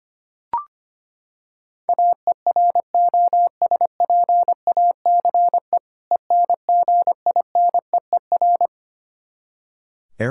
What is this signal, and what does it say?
Morse code, heard as a keyed tone, about 25 wpm, 700 Hz. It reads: AEROSPACE ENGINEER
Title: K1USN SST QSO Practice 25wpm